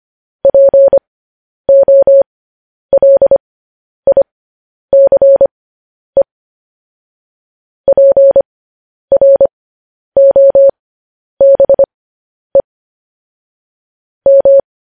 Morse code, heard as a keyed tone, about 25 wpm, spaced as for 10 wpm, 550 Hz. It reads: POLICE PROBE M
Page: News Headlines in Morse Code at 10 WPM